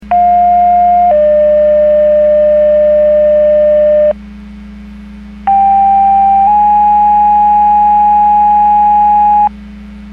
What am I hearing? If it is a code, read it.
Two-tone page, heard as a one-second tone, then a three-second tone, about 710 then 600 Hz, about 790 then 830 Hz. It is heard from 0.1 to 9.5 s.
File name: Portage / 00565 13-50 Portage 154.385 Paging